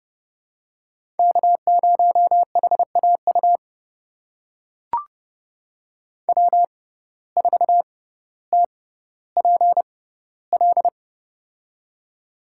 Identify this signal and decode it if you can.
Morse code, heard as a keyed tone, about 30 wpm, spaced as for 15 wpm, 700 Hz. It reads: K0HAU W4TPL
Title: USA Call Signs - 5 Characters - 6x Character Spacing - 30wpm